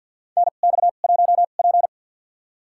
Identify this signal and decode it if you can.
Morse code, heard as a keyed tone, about 53 wpm, spaced as for 37 wpm, 700 Hz. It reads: NX1P
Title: USA Call Signs - 4 Characters - 2x Character Spacing - 50wpm